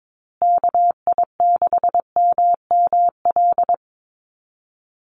Morse code, heard as a keyed tone, about 22 wpm, 690 Hz. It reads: KI6MML
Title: USA Call Signs Repeated 3x 22wpm